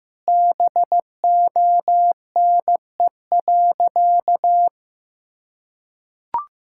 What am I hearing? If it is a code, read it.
Morse code, heard as a keyed tone, about 15 wpm, 700 Hz. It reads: BONE.
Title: Sentences from Top 1600 Words 15wpm